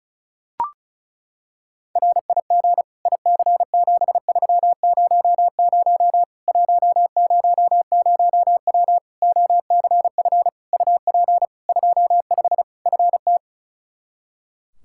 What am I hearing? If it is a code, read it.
Morse code, heard as a keyed tone, about 34 wpm, 700 Hz. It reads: RIG IC7300 100W OCF UP 25 FT
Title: Practice QSO - Exchange 2 - Info - 35wpm